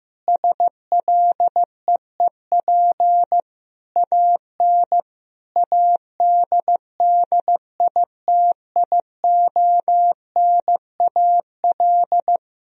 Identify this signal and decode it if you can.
Morse code, heard as a keyed tone, about 15 wpm, 700 Hz. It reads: SLEEP AN ADDITIONAL
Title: Sentences from Top 1000 Words - Review Part 4 - 15wpm